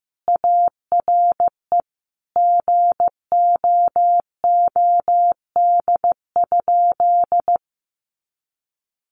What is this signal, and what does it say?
Morse code, heard as a keyed tone, about 15 wpm, 700 Hz. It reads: ARE GOOD?
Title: Sentences from Top 1300 Words 15wpm